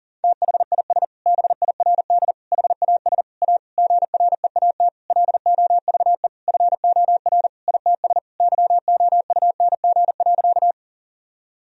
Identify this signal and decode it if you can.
Morse code, heard as a keyed tone, about 37 wpm, 700 Hz. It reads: THIS BIRD HAS A GREAT LOVE FOR ITS YOUNG.
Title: Sentences from Top 1000 Words - Review Part 2 - 40wpm